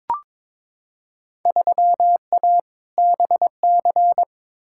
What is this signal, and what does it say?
Morse code, heard as a keyed tone, about 22 wpm, 700 Hz. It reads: 3A BC
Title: Field Day Exchanges - 22wpm